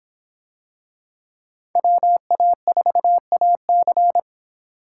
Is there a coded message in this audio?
Yes: Morse code from 1.8 to 4.2 s, heard as a keyed tone, about 24 wpm, 700 Hz. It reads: WA4AC